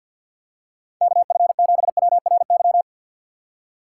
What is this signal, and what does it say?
Morse code, heard as a keyed tone, about 52 wpm, 700 Hz. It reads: KF7WRY
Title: USA Call Signs - 2x Character Spacing - 50wpm